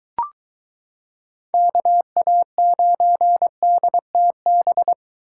Morse code, heard as a keyed tone, about 22 wpm, 700 Hz. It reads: KA9DTB